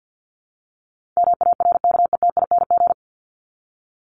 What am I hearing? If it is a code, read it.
Morse code, heard as a keyed tone, about 50 wpm, 700 Hz. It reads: MARKETING
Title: Top 751-1000 Words 50wpm